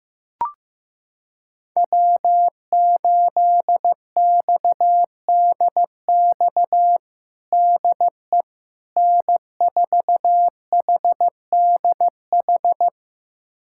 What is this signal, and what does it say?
Morse code, heard as a keyed tone, about 15 wpm, 700 Hz. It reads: W8XDX DE N4HDH
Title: USA Callsign DE Callsign - 5 Characters - 15wpm